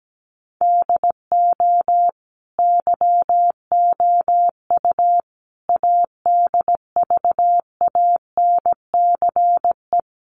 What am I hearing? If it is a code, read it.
Morse code, heard as a keyed tone, about 17 wpm, 700 Hz. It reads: DO YOU ADVANCE
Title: Sentences from Top 1400 Words 17wpm